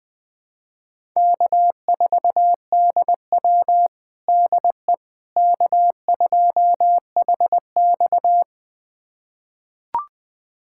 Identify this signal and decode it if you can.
Morse code, heard as a keyed tone, about 20 wpm, 700 Hz. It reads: K4DW DE K2HX
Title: USA Callsign DE Callsign - 4 Characters - 20wpm